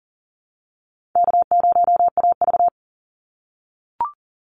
Morse code, heard as a keyed tone, about 40 wpm, 700 Hz. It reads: K0AV
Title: Call Signs - 40wpm